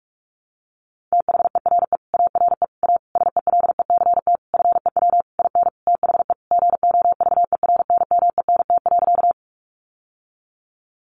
Morse code, heard as a keyed tone, about 44 wpm, 700 Hz. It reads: THERE ARE A SELECT FEW IN THE GOVERNMENT.
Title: Sentences from Top 1300 Words 45wpm